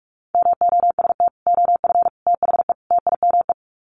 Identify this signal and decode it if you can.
Morse code, heard as a keyed tone, about 44 wpm, 700 Hz. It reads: MOST OF THE TIME
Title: Sets of 4 Words 45wpm